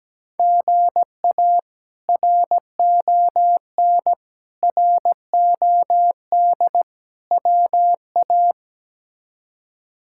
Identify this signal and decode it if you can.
Morse code, heard as a keyed tone, about 17 wpm, 700 Hz. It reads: GA RON ROD WA
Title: K1USN SST QSO Practice - 17wpm